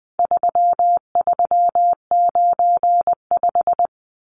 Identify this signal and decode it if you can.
Morse code, heard as a keyed tone, about 20 wpm, 700 Hz. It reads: 3395
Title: Numbers - 4-Digit 20wpm